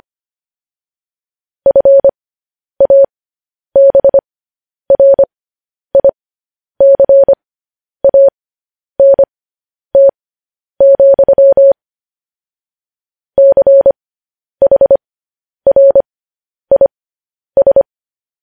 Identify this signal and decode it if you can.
Morse code, heard as a keyed tone, about 25 wpm, spaced as for 10 wpm, 550 Hz. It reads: FABRICANT, CHRIS
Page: news headlines in morse code at 10 wpm